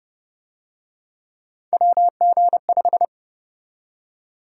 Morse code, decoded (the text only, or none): WG5